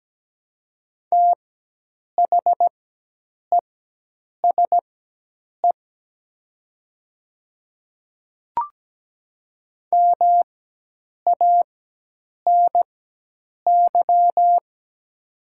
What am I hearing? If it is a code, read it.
Morse code, heard as a keyed tone, about 17 wpm, spaced as for 8 wpm, 700 Hz. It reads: THESE MANY
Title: 4x Character Spacing - Top 100 Words - 17wpm